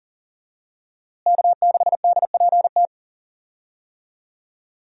Morse code, heard as a keyed tone, about 40 wpm, 700 Hz. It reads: K6DPT
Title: USA Call Signs - 5 Characters - Repeated 2x - 40wpm